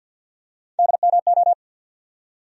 Morse code, heard as a keyed tone, about 51 wpm, 700 Hz. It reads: DMO
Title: USA Call Sign Postfixes - 3 Letters 50wpm